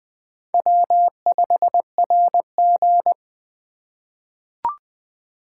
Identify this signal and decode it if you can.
Morse code, heard as a keyed tone, about 20 wpm, 700 Hz. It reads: W5RG
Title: USA Call Signs - 4 Characters - 3x Character Spacing - 20wpm